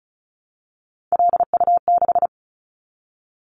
Morse code, heard as a keyed tone, about 36 wpm, 700 Hz. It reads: LU6
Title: Callsign Prefixes for Top 50 DXCC - 35wpm